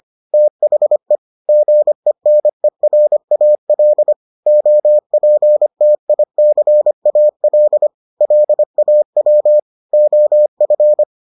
Morse code, heard as a keyed tone, about 25 wpm, 600 Hz. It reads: THE GENERAL OPTICAL LAW OF